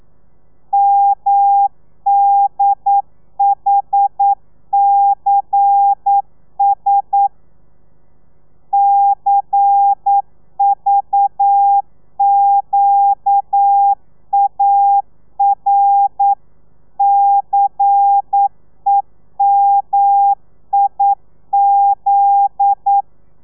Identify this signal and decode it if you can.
Morse code, heard as a keyed tone, about 9 wpm, 800 Hz. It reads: MDHCS CVQAR CEMIZ
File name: telegrafia / i2tfj / Lez11 40